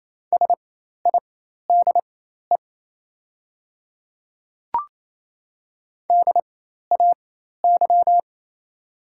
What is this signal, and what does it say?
Morse code, heard as a keyed tone, about 27 wpm, spaced as for 13 wpm, 690 Hz. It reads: SIDE DAY
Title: 4x Character Spacing - Top 300 Words - 28wpm